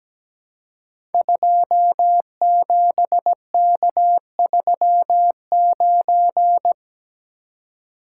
Morse code, heard as a keyed tone, about 17 wpm, 700 Hz. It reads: 27K39
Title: Code Groups - 5 Letter:Numbers - 17wpm